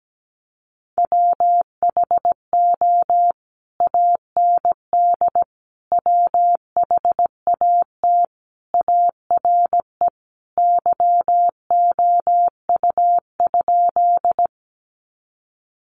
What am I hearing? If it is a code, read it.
Morse code, heard as a keyed tone, about 17 wpm, 700 Hz. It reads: WHO AND WHAT ARE YOU?
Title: Sentences from Top 100 Words 17wpm